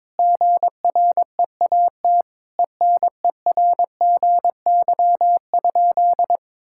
Morse code, heard as a keyed tone, about 22 wpm, 700 Hz. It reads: GREAT ENERGY?
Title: Sentences from Top 1000 Words - Review Part 6 - 22wpm